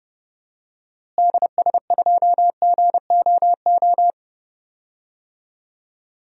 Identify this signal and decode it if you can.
Morse code, heard as a keyed tone, about 30 wpm, 700 Hz. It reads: DS2GOO
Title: DXCC-100 Call Signs - 6 Characters - Repeated 3x - 30wpm